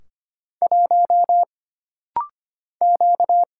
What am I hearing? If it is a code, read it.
Morse code, heard as a keyed tone, about 25 wpm, 700 Hz. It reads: 1 Q
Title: Single Letter-Number Alt - Rapid-Fire 25wpm